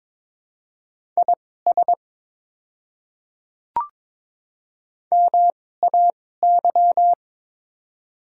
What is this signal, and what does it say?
Morse code, heard as a keyed tone, about 22 wpm, spaced as for 16 wpm, 700 Hz. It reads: IS MAY
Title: 2x Character Spacing - Top 100 Words - 22wpm